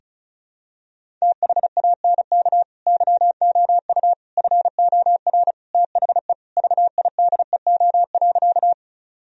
Morse code, heard as a keyed tone, about 36 wpm, 700 Hz. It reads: THANK YOU FOR THE VIDEO.
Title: Sentences from Top 1000 Words - Review Part 4 - 35wpm